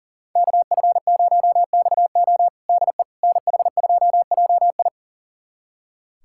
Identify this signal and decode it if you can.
Morse code, heard as a keyed tone, about 40 wpm, 700 Hz. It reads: KF0XO DE NH2JI
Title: USA Callsign DE Callsign - 5 Characters - 40wpm